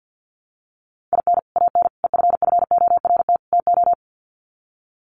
Morse code, heard as a keyed tone, about 49 wpm, 700 Hz. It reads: IN AN EFFORT TO
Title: Sets of 4 Words 50wpm